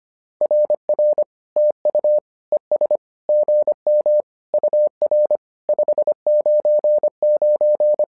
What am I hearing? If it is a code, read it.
Morse code, heard as a keyed tone, about 25 wpm, 600 Hz. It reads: RR TU ES GM UR 599
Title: SOTA QSO Practice 25wpm